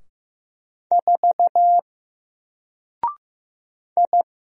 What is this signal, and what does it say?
Morse code, heard as a keyed tone, about 15 wpm, 700 Hz. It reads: 4 I
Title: Single Character (Letter:Number:Punctuation:Prosign) 15wpm